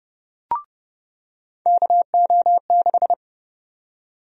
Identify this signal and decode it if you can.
Morse code, heard as a keyed tone, about 30 wpm, 700 Hz. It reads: KO6
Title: USA Call Sign Prefixes - 30wpm